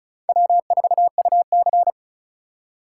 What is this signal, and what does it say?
Morse code, heard as a keyed tone, about 34 wpm, 700 Hz. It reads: W4UC
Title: USA Call Signs - 4 Characters - 3x Character Spacing - 35wpm